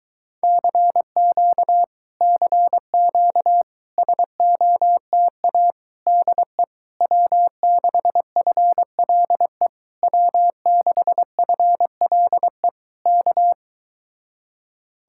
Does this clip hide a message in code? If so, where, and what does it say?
Yes: Morse code from 0.4 to 13.5 s, heard as a keyed tone, about 23 wpm, 700 Hz. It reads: CQ CQ SOTA DE W6FLE W6FLE K